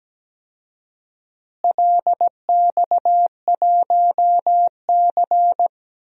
Morse code, heard as a keyed tone, about 17 wpm, 700 Hz. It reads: LX1C
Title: DXCC-100 Call Signs - 4 Characters - Repeated 3x - 17wpm